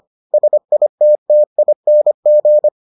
Morse code, heard as a keyed tone, about 25 wpm, 600 Hz. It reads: SITTING